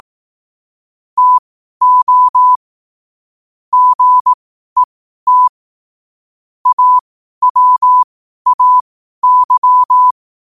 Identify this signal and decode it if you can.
Morse code, heard as a keyed tone, about 18 wpm, spaced as for 12 wpm, 1000 Hz. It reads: TO GET AWAY
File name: qotdfiles / 12wpm